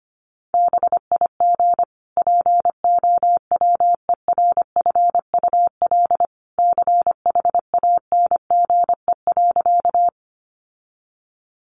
Morse code, heard as a keyed tone, about 25 wpm, 700 Hz. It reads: BIG POWERFUL CHANGE.
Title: Sentences from Top 1300 Words 25wpm